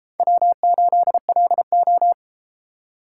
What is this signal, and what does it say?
Morse code, heard as a keyed tone, about 33 wpm, 700 Hz. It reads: W8LO